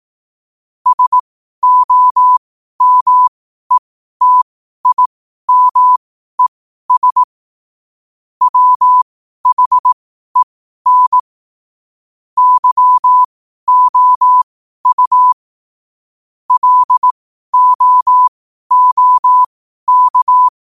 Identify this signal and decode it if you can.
Morse code, heard as a keyed tone, about 18 wpm, spaced as for 12 wpm, 1000 Hz. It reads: SOMETIMES WHEN YOU LOOK
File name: qotdfiles / 12wpm